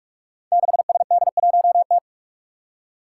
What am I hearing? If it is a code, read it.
Morse code, heard as a keyed tone, about 44 wpm, 700 Hz. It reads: 6SD1T
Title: Five Challenging Letter-Number Combinations 45wpm